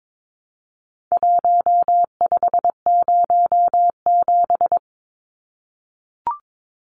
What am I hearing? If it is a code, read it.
Morse code, heard as a keyed tone, about 22 wpm, 700 Hz. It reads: 1507
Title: Numbers - Contest Style - 4-Digit 22wpm